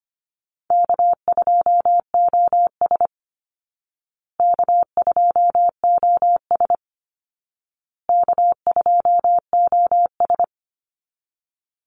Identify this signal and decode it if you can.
Morse code, heard as a keyed tone, about 25 wpm, 700 Hz. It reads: K2OS K2OS K2OS
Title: Easy Call Signs Repeated 3x - 25wpm